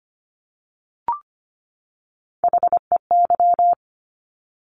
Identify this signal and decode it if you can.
Morse code, heard as a keyed tone, about 25 wpm, 700 Hz. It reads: HEY
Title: Top Three Letter Combinations in Words 25wpm